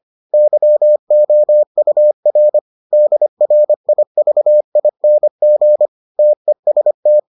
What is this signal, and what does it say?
Morse code, heard as a keyed tone, about 25 wpm, 600 Hz. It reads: YOUR DRIVING TEST